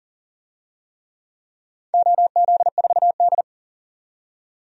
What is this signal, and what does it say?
Morse code, heard as a keyed tone, about 40 wpm, 700 Hz. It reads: OZ4D
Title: DXCC-100 Call Signs - 4 Characters - 40wpm